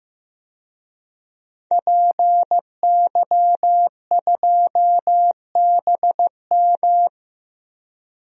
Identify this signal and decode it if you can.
Morse code, heard as a keyed tone, about 15 wpm, 700 Hz. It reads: PY2BM
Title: DXCC-100 Call Signs - 5 Characters - Repeated 3x - 15wpm